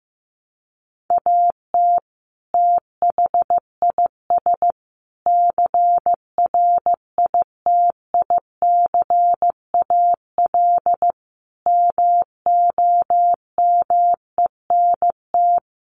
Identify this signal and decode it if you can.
Morse code, heard as a keyed tone, about 15 wpm, 700 Hz. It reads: AT THIS CRITICAL MOMENT
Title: Sentences from Top 1100 Words 15wpm